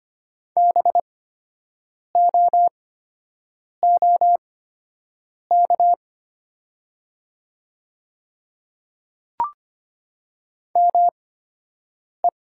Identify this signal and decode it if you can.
Morse code, heard as a keyed tone, about 25 wpm, spaced as for 7 wpm, 690 Hz. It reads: BOOK ME
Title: 8x Character Spacing - Top 500 Words - 25wpm